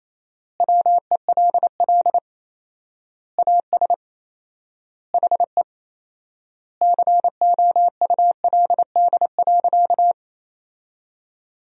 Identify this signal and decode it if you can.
Morse code, heard as a keyed tone, about 27 wpm, 700 Hz. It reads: WELL AS HE COULD.
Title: 4x Word-Spacing - Sentences from Top 500 Words - 28wpm